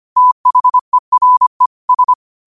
Morse code, heard as a keyed tone, about 25 wpm, 1000 Hz. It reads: THERE S
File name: qotdfiles / 25wpm